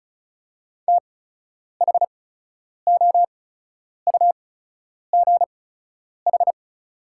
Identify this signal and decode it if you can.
Morse code, heard as a keyed tone, about 36 wpm, spaced as for 10 wpm, 690 Hz. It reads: THOUGH
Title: 8x Character Spacing - Top 500 Words - 35wpm